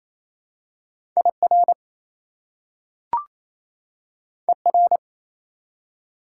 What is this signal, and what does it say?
Morse code, heard as a keyed tone, about 28 wpm, 700 Hz. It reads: IR ER
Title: Prefixes and Suffixes 28wpm